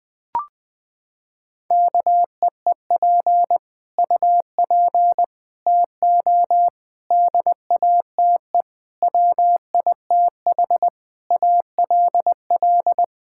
Morse code, heard as a keyed tone, about 19 wpm, 700 Hz. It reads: KEEP UP TO DATE WITH ALL
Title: Sets of 9 Words - 20wpm